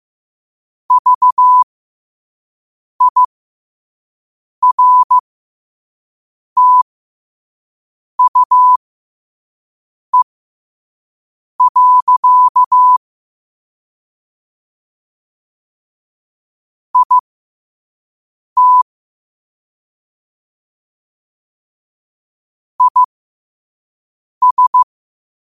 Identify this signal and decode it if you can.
Morse code, heard as a keyed tone, about 15 wpm, spaced as for 5 wpm, 1000 Hz. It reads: VIRTUE. IT IS
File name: qotdfiles / 5wpm